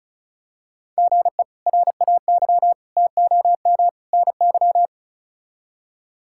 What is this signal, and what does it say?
Morse code, heard as a keyed tone, about 36 wpm, 700 Hz. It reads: GE RAY TOM NY
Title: K1USN SST QSO Practice - 35wpm